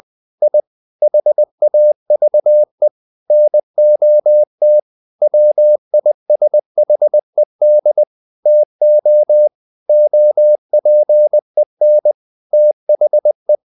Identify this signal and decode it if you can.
Morse code, heard as a keyed tone, about 20 wpm, 600 Hz. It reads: I HAVE NOT WISHED TO OPEN THE
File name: the old man and the sea / the old man and the sea part 1 20wpm